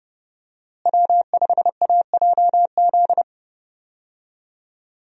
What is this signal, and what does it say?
Morse code, heard as a keyed tone, about 30 wpm, 700 Hz. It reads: W5AJZ
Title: USA Call Signs - 5 Characters - Repeated 3x - 30wpm